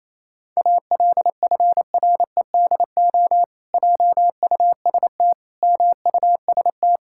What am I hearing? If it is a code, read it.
Morse code, heard as a keyed tone, about 28 wpm, 700 Hz. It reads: ALFREDO JUST MUST
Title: Pangram Challenge v2 - 28wpm